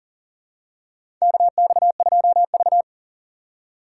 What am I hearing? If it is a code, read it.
Morse code, heard as a keyed tone, about 38 wpm, 700 Hz. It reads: KX2V
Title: USA Call Signs - 4 Characters - 3x Character Spacing - 40wpm